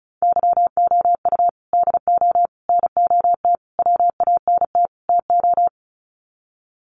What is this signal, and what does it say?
Morse code, heard as a keyed tone, about 34 wpm, 700 Hz. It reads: YOU DO NOT WANT TO
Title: Sets of 5 Words 35wpm